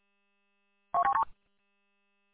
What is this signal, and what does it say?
There is no code.